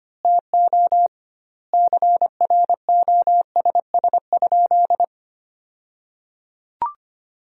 Morse code, heard as a keyed tone, about 25 wpm, 700 Hz. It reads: TO CROSS?
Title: 2x Word-Spacing - Sentences from Top 300 Words - 25wpm